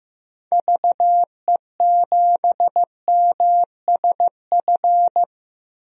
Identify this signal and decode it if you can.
Morse code, heard as a keyed tone, about 15 wpm, 700 Hz. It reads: VE7MSF
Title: DXCC-100 Call Signs - 6 Characters - Repeated 2x - 15wpm